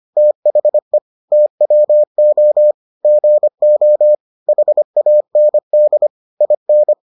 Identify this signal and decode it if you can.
Morse code, heard as a keyed tone, about 25 wpm, 600 Hz. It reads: THE TWO GO HAND IN